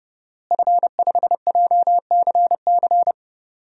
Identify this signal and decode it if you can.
Morse code, heard as a keyed tone, about 28 wpm, 700 Hz. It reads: F5JCC
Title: DXCC-100 Call Signs - 5 Characters - Repeated 3x - 30wpm